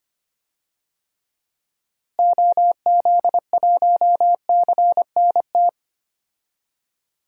Morse code, heard as a keyed tone, about 25 wpm, 700 Hz. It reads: OZ1CNT
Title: DXCC-100 Call Signs - 6 Characters - Repeated 2x - 25wpm